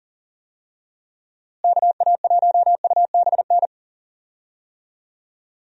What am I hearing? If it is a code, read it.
Morse code, heard as a keyed tone, about 38 wpm, 700 Hz. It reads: KA1UBN
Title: USA Call Signs - 6 Characters - Repeated 2x - 40wpm